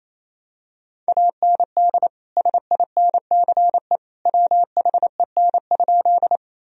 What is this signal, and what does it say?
Morse code, heard as a keyed tone, about 28 wpm, 700 Hz. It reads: AND SINCE WHEN?
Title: Sentences from Top 1000 Words - Review Part 1 - 28wpm